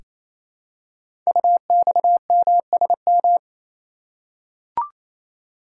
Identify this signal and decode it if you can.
Morse code, heard as a keyed tone, about 29 wpm, 700 Hz. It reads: UXMSM
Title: Code Groups - 5 Letters - 28wpm